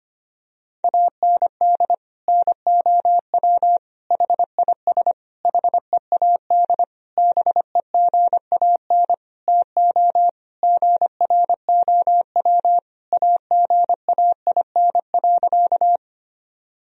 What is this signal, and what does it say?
Morse code, heard as a keyed tone, about 25 wpm, 700 Hz. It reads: AND NOW HIS HEAD BEGAN TO GROW AGAIN.
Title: Sentences from Top 1000 Words - Review Part 1 - 25wpm